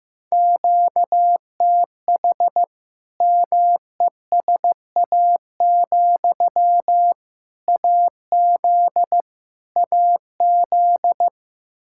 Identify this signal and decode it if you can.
Morse code, heard as a keyed tone, about 15 wpm, 700 Hz. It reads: QTH MESA, AZ AZ
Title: Practice QSO - Exchange 1 - Beginning - 15wpm